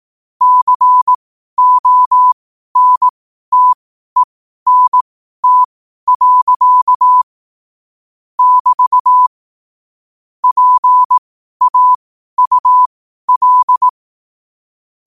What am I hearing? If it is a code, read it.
Morse code, heard as a keyed tone, about 18 wpm, spaced as for 12 wpm, 1000 Hz. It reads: CONTENT. <BT> PAUL
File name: qotdfiles / 12wpm